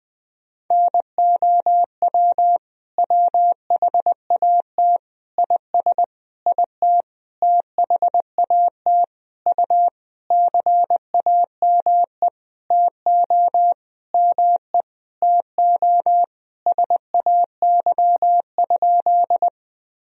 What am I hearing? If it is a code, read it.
Morse code, heard as a keyed tone, about 21 wpm, 700 Hz. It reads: NOW WHAT IS IT THAT U CAME TO ME TO SAY?
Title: Sentences from Top 200 Words and Abbreviations 20wpm